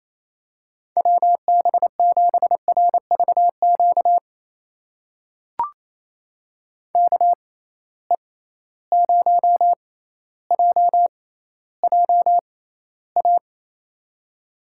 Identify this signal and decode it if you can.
Morse code, heard as a keyed tone, about 28 wpm, spaced as for 14 wpm, 700 Hz. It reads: WB7RVQ KE0JJA
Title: USA Call Signs - 6 Characters - 6x Character Spacing - 28wpm